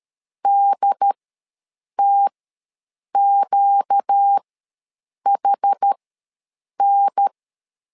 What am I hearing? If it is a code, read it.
Morse code, heard as a keyed tone, about 13 wpm, spaced as for 7 wpm, 800 Hz. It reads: DTQHN
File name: LEZIONI / 4 Q ws 12wpm 1912